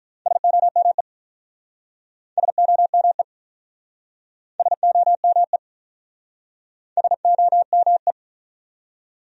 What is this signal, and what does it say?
Morse code, heard as a keyed tone, about 47 wpm, 700 Hz. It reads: SOME SOME SOME SOME